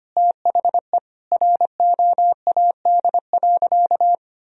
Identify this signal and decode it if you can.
Morse code, heard as a keyed tone, about 25 wpm, 700 Hz. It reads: THE ROAD.
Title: Sentences from Top 1000 Words - Review Part 2 - 25wpm